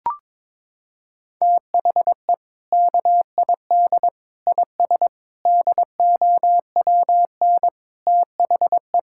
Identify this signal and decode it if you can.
Morse code, heard as a keyed tone, about 22 wpm, 700 Hz. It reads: THE KID IS DOWN THE
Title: Sentences from Top 1600 Words 22wpm